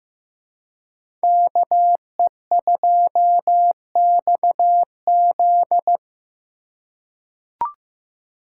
Morse code, heard as a keyed tone, about 15 wpm, 700 Hz. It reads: KE2XZ
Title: USA Call Signs - 5 Characters - 4x Character Spacing - 15wpm